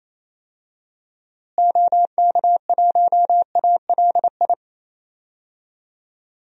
Morse code, heard as a keyed tone, about 28 wpm, 700 Hz. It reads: OK1ALI
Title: DXCC-100 Call Signs - 6 Characters - Repeated 3x - 28wpm